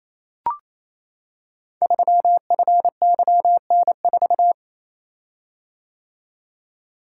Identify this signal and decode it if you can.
Morse code, heard as a keyed tone, about 27 wpm, 700 Hz. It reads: 3FYN4
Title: Code Groups - 5 Letter:Numbers - 28wpm